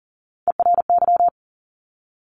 Morse code, heard as a keyed tone, about 42 wpm, 700 Hz. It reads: ERY
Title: Top Three Letter Combinations in Words 40wpm